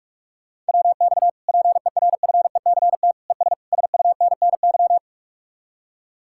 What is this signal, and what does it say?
Morse code, heard as a keyed tone, about 65 wpm, spaced as for 55 wpm, 700 Hz. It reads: WX PERFECT ES SUNNY
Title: Speed-Racing Common QSO Phrases - 45wpm